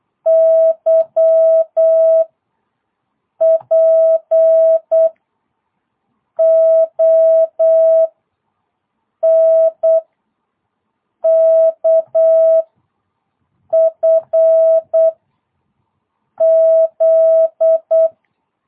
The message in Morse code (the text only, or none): Y P O N K F Z